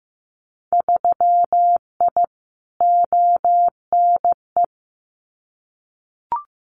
Morse code, heard as a keyed tone, about 15 wpm, 700 Hz. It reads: 3I ONE
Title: Winter Field Day Exchanges Encore 15wpm